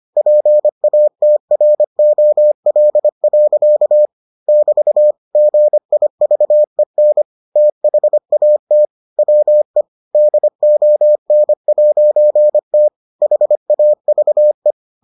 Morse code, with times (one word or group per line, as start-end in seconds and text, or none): 0.2-4.1 PATROL.
4.5-5.1 <BT>
5.4-7.2 GIVEN
7.6-8.9 THAT
9.2-9.8 WE
10.1-12.9 DON'T
13.2-14.7 HAVE